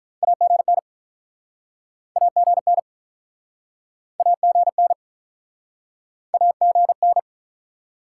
Morse code, heard as a keyed tone, about 48 wpm, 700 Hz. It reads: AGN AGN AGN AGN